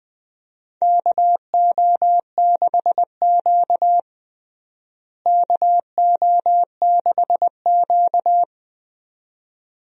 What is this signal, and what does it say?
Morse code, heard as a keyed tone, about 20 wpm, 700 Hz. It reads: KO6Q KO6Q
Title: USA Call Signs - 4 Characters - Repeated 2x - 20wpm